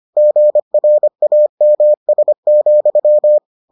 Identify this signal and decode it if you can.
Morse code, heard as a keyed tone, about 25 wpm, 600 Hz. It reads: GRAMS,